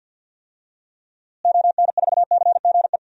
Morse code, heard as a keyed tone, about 47 wpm, 700 Hz. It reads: ON4CGE
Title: DXCC-100 Call Signs - 6 Characters - Repeated 2x - 50wpm